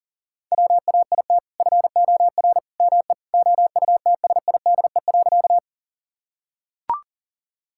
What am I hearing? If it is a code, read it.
Morse code, heard as a keyed tone, about 40 wpm, 700 Hz. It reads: WAIT FOR ME OUTSIDE.
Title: Sentences from Top 1000 Words - Review Part 6 - 40wpm